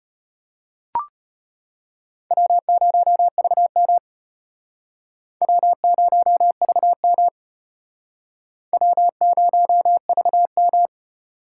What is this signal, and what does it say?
Morse code, heard as a keyed tone, about 34 wpm, 700 Hz. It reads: W0VM W0VM W0VM